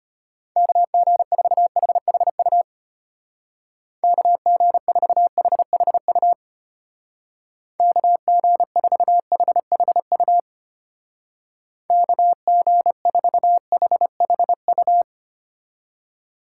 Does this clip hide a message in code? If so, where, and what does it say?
Yes: Morse code from 0.6 to 15.0 s, heard as a keyed tone, about 32 wpm, 700 Hz. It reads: KG4HHU KG4HHU KG4HHU KG4HHU